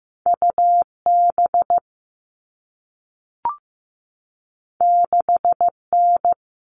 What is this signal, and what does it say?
Morse code, heard as a keyed tone, about 15 wpm, 700 Hz. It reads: UB 6N
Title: All Cut Numbers - 2-Digit 15wpm